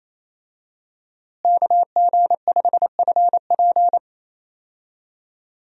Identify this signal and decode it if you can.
Morse code, heard as a keyed tone, about 28 wpm, 700 Hz. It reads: KG5FP
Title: USA Call Signs - 5 Characters - Repeated 3x - 28wpm